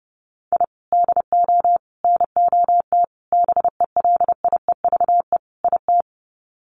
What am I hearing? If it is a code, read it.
Morse code, heard as a keyed tone, about 30 wpm, 700 Hz. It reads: I DO NOT BELIEVE IT
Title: Sets of 5 Words 30wpm